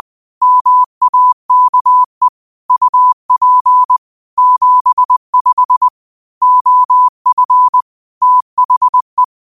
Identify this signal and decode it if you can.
Morse code, heard as a keyed tone, about 20 wpm, 1000 Hz. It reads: MAKE UP 75 OF THE
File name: qotdfiles / 20wpm